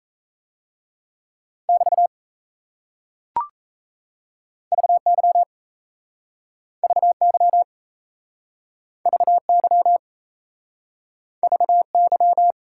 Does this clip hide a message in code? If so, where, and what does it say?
Yes: Morse code from 1.7 to 12.5 s, heard as a keyed tone, about 37 wpm, 700 Hz. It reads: <BT> VY VY VY VY